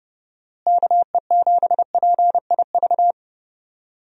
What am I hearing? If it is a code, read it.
Morse code, heard as a keyed tone, about 30 wpm, 700 Hz. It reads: KE7PIV
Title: USA Call Signs - 6 Characters - Repeated 3x - 30wpm